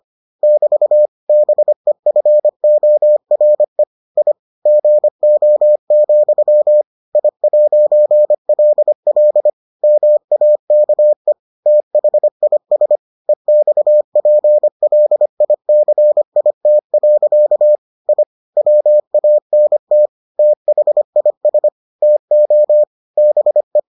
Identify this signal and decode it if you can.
Morse code, heard as a keyed tone, about 25 wpm, 600 Hz. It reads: <BT> BEFORE I GO, I'LL MAKE THIS EXPLICIT. I WANT THIS TO BE